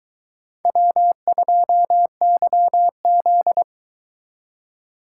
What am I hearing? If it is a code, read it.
Morse code, heard as a keyed tone, about 22 wpm, 700 Hz. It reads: W2YZ